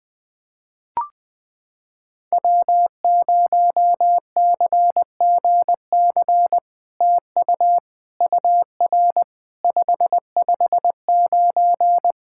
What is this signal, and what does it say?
Morse code, heard as a keyed tone, about 21 wpm, 700 Hz. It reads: W0CGC TU UR 559
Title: POTA QSO Practice - 20wpm